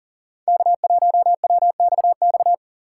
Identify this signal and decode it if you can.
Morse code, heard as a keyed tone, about 37 wpm, 700 Hz. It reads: K1WXX
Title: USA Call Signs - 5 Characters - 2x Character Spacing - 40wpm